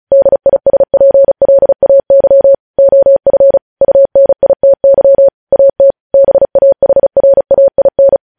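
Morse code, heard as a keyed tone, about 36 wpm, 550 Hz. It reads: DISPLAY OF UNITY AT BAHRAIN
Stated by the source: Morse practice at 35 wpm sending news headlines.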